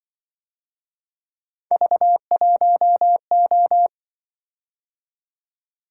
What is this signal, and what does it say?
Morse code, heard as a keyed tone, about 24 wpm, 700 Hz. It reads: V1O
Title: Speed-Racing - Three Letter-Number Combinations - 20wpm